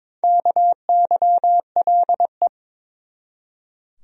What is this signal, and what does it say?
Morse code, heard as a keyed tone, about 21 wpm, 700 Hz. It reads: KYLE
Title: USA Common Names - Max 5 Letters - 22wpm